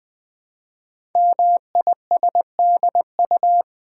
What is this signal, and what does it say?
Morse code, heard as a keyed tone, about 20 wpm, 700 Hz. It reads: MISDU
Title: Code Groups - 5 Letters - 20wpm